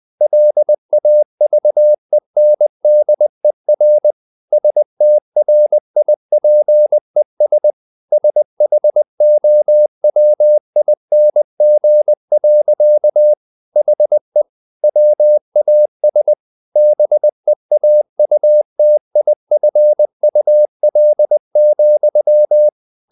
Morse code, heard as a keyed tone, about 20 wpm, 600 Hz. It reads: LAVENDER STRIPES SHOWING. HE WAS BEAUTIFUL,
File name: the old man and the sea / the old man and the sea part 1 20wpm